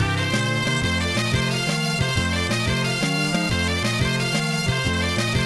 There is no code.